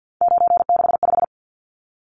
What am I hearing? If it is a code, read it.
Morse code, heard as a keyed tone, about 50 wpm, 700 Hz. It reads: 965
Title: Numbers - 3-Digit 50wpm